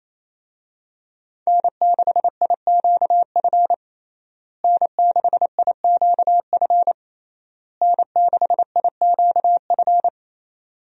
Morse code, heard as a keyed tone, about 28 wpm, 700 Hz. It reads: N6IQF N6IQF N6IQF
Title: USA Call Signs - 5 Characters - Repeated 3x - 28wpm